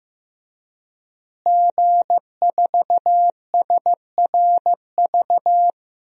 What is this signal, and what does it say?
Morse code, heard as a keyed tone, about 16 wpm, 700 Hz. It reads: G4SRV
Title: DXCC-100 Call Signs - 5 Characters - Repeated 3x - 15wpm